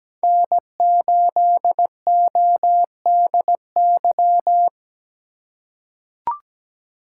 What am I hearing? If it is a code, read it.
Morse code, heard as a keyed tone, about 17 wpm, 700 Hz. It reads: N8ODY
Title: USA Call Signs - 5 Characters - 8x Character Spacing - 17wpm